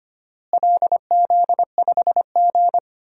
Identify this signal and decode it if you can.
Morse code, heard as a keyed tone, about 25 wpm, 700 Hz. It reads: LZ5G
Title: DXCC-100 Call Signs - 4 Characters - Repeated 3x - 25wpm